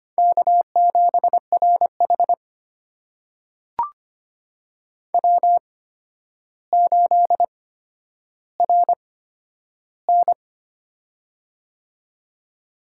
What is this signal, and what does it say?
Morse code, heard as a keyed tone, about 25 wpm, spaced as for 11 wpm, 700 Hz. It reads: K7RH W8RN
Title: USA Call Signs - 4 Characters - 8x Character Spacing - 25wpm